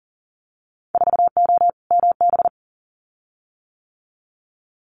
Morse code, heard as a keyed tone, about 40 wpm, 700 Hz. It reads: 4O MB
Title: Winter Field Day Exchanges Encore 40wpm